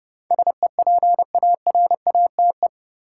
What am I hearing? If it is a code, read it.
Morse code, heard as a keyed tone, about 31 wpm, 700 Hz. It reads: SEPARATE
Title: Sentences from Top 1000 Words - Review Part 7 - New Word Drill - 30wpm